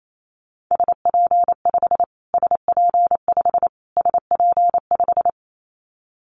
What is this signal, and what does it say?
Morse code, heard as a keyed tone, about 28 wpm, 700 Hz. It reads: SP5 SP5 SP5
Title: Callsign Prefixes for Top 25 DXCC - Repeated 3x - 28wpm